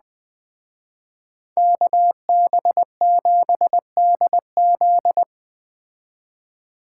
Morse code, heard as a keyed tone, about 20 wpm, 700 Hz. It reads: KB7DZ
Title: Speed Racing - USA 2x2 Call Signs 17wpm